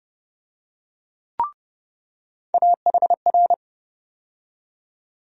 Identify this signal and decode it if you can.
Morse code, heard as a keyed tone, about 30 wpm, 700 Hz. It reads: AHR
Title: USA Call Sign Postfixes - 3 Letters 30wpm